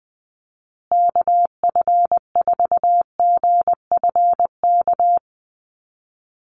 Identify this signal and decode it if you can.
Morse code, heard as a keyed tone, about 20 wpm, 700 Hz. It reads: KF4GFK
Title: USA Call Signs Repeated 2x 20wpm